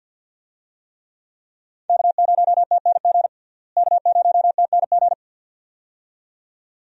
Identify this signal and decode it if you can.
Morse code, heard as a keyed tone, about 51 wpm, 700 Hz. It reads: K0TNG K0TNG
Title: USA Call Signs - 5 Characters - Repeated 2x - 50wpm